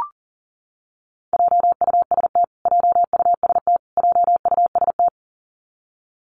Morse code, heard as a keyed tone, about 40 wpm, 700 Hz. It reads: JUST JUST JUST
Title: Top 300 Words Repeated 40wpm